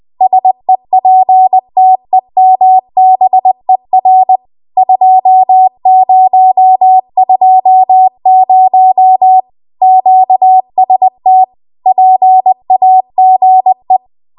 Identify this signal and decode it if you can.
Morse code, heard as a keyed tone, about 20 wpm, 750 Hz. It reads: SEPTEMBER 2020 QST PAGE
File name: Archive / 20 WPM / 20WPM